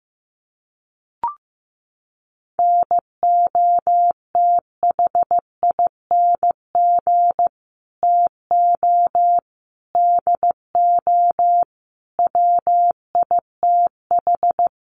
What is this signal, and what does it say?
Morse code, heard as a keyed tone, about 15 wpm, 700 Hz. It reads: NOTHING TO DO WITH
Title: Sets of 5 Words 15wpm